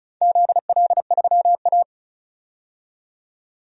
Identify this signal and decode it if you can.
Morse code, heard as a keyed tone, about 34 wpm, 700 Hz. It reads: ZL3A
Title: DXCC-100 Call Signs - 4 Characters - 35wpm